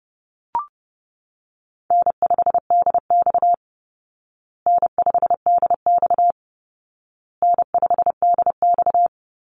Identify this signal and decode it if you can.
Morse code, heard as a keyed tone, about 28 wpm, 700 Hz. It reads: N5DX N5DX N5DX
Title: Easy Call Signs Repeated 3x - 30wpm